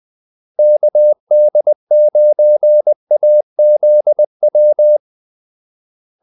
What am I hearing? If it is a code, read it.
Morse code, heard as a keyed tone, about 20 wpm, 600 Hz. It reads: KD9AZW
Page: Kaldesignaler 20 wpm